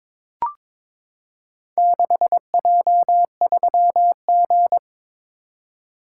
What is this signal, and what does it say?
Morse code, heard as a keyed tone, about 22 wpm, 700 Hz. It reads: 6J3G
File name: practice / 22wpm / Four Letter-Number Combinations - 22wpm